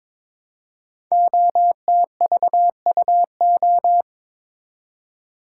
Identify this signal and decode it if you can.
Morse code, heard as a keyed tone, about 22 wpm, 700 Hz. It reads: OTVUO
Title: Code Groups - 5 Letter:Numbers - 22wpm